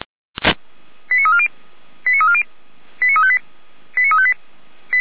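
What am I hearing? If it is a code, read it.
Five-tone selective call (ZVEI), heard as quick runs of five tones, tones of about 70 ms. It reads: 89340 89340 89358 89358
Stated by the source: ZVEI five-tone selective call received via SDR, from the Feuerwehr Orlamünde fire brigade audio server.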